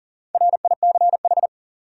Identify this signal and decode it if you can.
Morse code, heard as a keyed tone, about 40 wpm, 700 Hz. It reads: RICH
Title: USA Common Names - Max 5 Letters - 40wpm